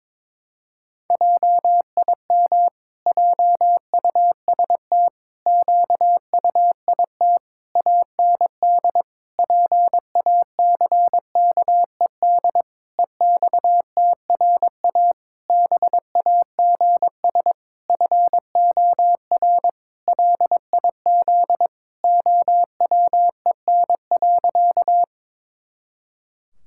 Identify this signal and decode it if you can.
Morse code, heard as a keyed tone, about 22 wpm, 700 Hz. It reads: JIM JUST QUIT AND PACKED EXTRA BAGS FOR LIZ OWEN.
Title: Pangram Challenge v2 - 22wpm